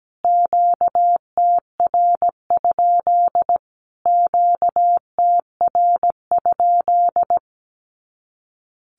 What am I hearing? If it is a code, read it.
Morse code, heard as a keyed tone, about 17 wpm, 700 Hz. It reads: QTR? QTR?
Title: Q-Codes Repeated 3x 17wpm